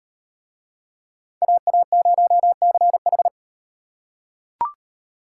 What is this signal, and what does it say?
Morse code, heard as a keyed tone, about 36 wpm, 700 Hz. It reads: AA0CH